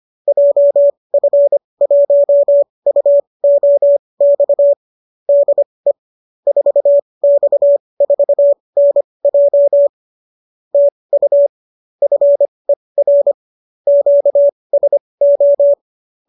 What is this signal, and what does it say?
Morse code, heard as a keyed tone, about 25 wpm, spaced as for 20 wpm, 570 Hz. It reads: JF1UOX DE 4X4NJ TU FER QSO